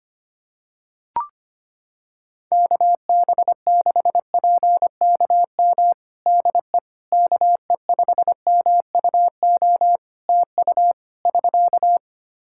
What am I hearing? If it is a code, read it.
Morse code, heard as a keyed tone, about 25 wpm, 700 Hz. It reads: KB6PKM DE KE5MUO TU <SK>
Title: Practice QSO - Exchange 4 - Ending - 25wpm